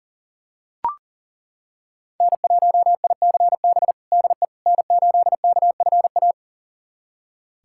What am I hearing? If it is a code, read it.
Morse code, heard as a keyed tone, about 40 wpm, 700 Hz. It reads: N1ICB DE N8KFA
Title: USA Callsign DE Callsign - 5 Characters - 40wpm